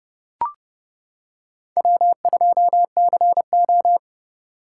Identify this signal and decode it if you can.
Morse code, heard as a keyed tone, about 29 wpm, 700 Hz. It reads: W2CO